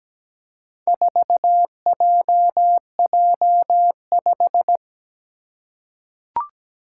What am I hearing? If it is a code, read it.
Morse code, heard as a keyed tone, about 17 wpm, 700 Hz. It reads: 4JJ5
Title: Four Challenging Letter-Number Combinations 17wpm